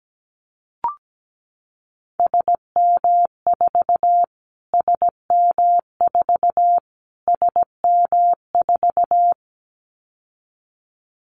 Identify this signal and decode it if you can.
Morse code, heard as a keyed tone, about 17 wpm, 700 Hz. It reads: SM4 SM4 SM4
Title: Callsign Prefixes for Top 25 DXCC - Repeated 3x - 17wpm